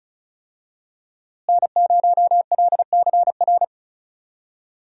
Morse code, heard as a keyed tone, about 36 wpm, 700 Hz. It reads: N0LCR
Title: USA Call Signs - 5 Characters - Repeated 2x - 35wpm